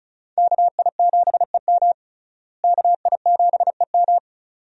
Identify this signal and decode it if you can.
Morse code, heard as a keyed tone, about 36 wpm, 700 Hz. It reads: KI7EM KI7EM
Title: USA Call Signs - 5 Characters - Repeated 2x - 35wpm